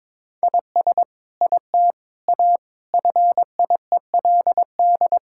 Morse code, heard as a keyed tone, about 21 wpm, 700 Hz. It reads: IS IT A FIELD
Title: Sentences from Top 1000 Words - Review Part 5 - 22wpm